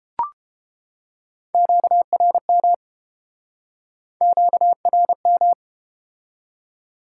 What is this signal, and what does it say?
Morse code, heard as a keyed tone, about 32 wpm, 700 Hz. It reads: QRM QRM